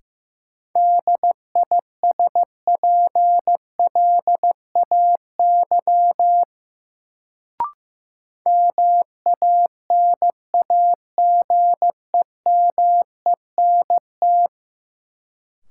Morse code, heard as a keyed tone, about 15 wpm, 700 Hz. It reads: DISPLAY MANAGEMENT
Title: Sentences from Top 1200 Words - New Word Drill Alt - 15wpm